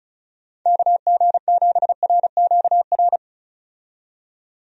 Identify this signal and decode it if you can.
Morse code, heard as a keyed tone, about 36 wpm, 700 Hz. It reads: KG7RQR
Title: USA Call Signs - 6 Characters - 35wpm